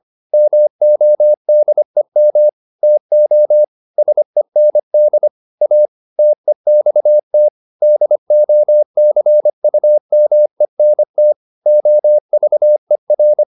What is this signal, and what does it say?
Morse code, heard as a keyed tone, about 25 wpm, 600 Hz. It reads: MODEM TO SEND A TEXT DOCUMENT OVER